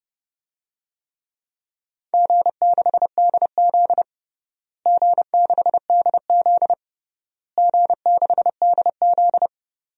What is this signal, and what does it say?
Morse code, heard as a keyed tone, about 32 wpm, 700 Hz. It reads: G6DZ G6DZ G6DZ
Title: DXCC-100 Call Signs - 4 Characters - Repeated 3x - 30wpm